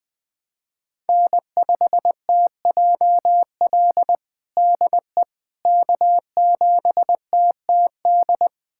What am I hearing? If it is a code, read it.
Morse code, heard as a keyed tone, about 21 wpm, 700 Hz. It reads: N5TJL DE K7TTD
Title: USA Callsign DE Callsign - 5 Characters - 20wpm